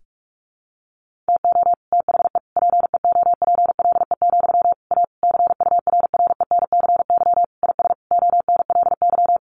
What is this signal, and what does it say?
Morse code, heard as a keyed tone, about 44 wpm, 700 Hz. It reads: TO THE PEOPLE, A CURRENCY IS ONLY
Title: Sentences from Top 1500 Words 45wpm